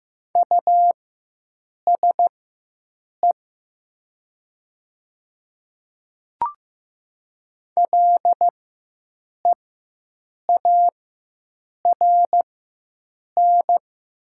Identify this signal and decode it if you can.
Morse code, heard as a keyed tone, about 15 wpm, spaced as for 7 wpm, 700 Hz. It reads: USE LEARN
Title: 4x Character Spacing - Top 300 Words - 15wpm